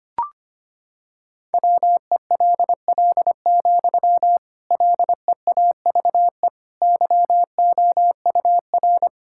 Morse code, heard as a keyed tone, about 25 wpm, 700 Hz. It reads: WELL, LEAVE YOUR
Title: Sentences from Top 1600 Words 25wpm